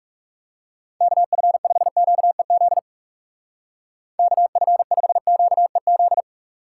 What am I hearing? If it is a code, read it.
Morse code, heard as a keyed tone, about 37 wpm, 700 Hz. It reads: KF5QEZ KF5QEZ
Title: Speed Racing - USA 2x3 Call Signs 30wpm